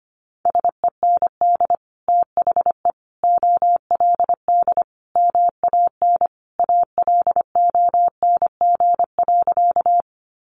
Morse code, heard as a keyed tone, about 25 wpm, 700 Hz. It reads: SEND THE OLD MAN ALONG.
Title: Sentences from Top 1500 Words 25wpm